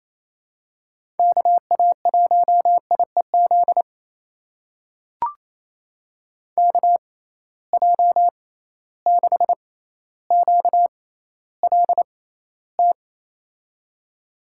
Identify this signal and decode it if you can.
Morse code, heard as a keyed tone, about 28 wpm, spaced as for 14 wpm, 700 Hz. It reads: KA1IEZ KJ6QLT
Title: USA Call Signs - 6 Characters - 6x Character Spacing - 28wpm